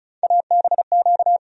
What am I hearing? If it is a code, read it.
Morse code, heard as a keyed tone, about 36 wpm, 700 Hz. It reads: ABQ
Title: USA Call Sign Postfixes - 3 Letters 35wpm